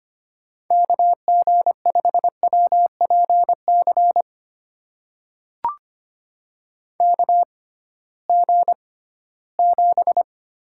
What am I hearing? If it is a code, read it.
Morse code, heard as a keyed tone, about 25 wpm, spaced as for 16 wpm, 700 Hz. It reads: KG5WPC KG7
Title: USA Call Signs - 6 Characters - 6x Character Spacing - 25wpm